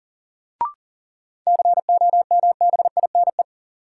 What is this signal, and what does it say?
Morse code, heard as a keyed tone, about 39 wpm, 700 Hz. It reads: COMBINE
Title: Sentences from Top 1600 Words - New Word Drill - 40wpm